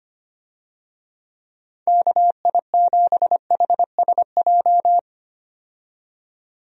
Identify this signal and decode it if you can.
Morse code, heard as a keyed tone, about 25 wpm, 700 Hz. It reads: KI7HSJ
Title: USA Call Signs - 6 Characters - 25wpm